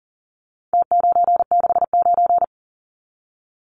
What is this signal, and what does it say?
Morse code, heard as a keyed tone, about 40 wpm, 700 Hz. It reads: T969
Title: Numbers - Contest Style - 4-Digit 40wpm